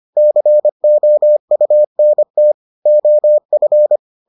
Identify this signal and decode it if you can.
Morse code, heard as a keyed tone, about 25 wpm, 600 Hz. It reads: COUNT OF